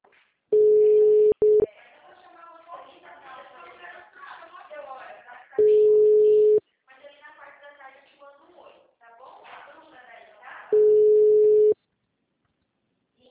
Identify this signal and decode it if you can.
Ringback tone (Europe, 425 Hz), with speech.